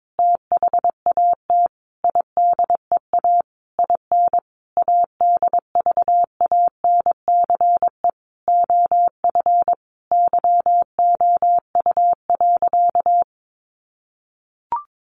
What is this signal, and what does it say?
Morse code, heard as a keyed tone, about 22 wpm, 700 Hz. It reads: THAT IDEA IN ADVANCE OF YOU.
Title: Sentences from Top 1400 Words 22wpm